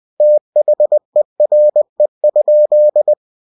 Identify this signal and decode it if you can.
Morse code, heard as a keyed tone, about 20 wpm, 600 Hz. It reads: THERE?
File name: cw-books / winnie-the-pooh / winnie-20wpm-page-0068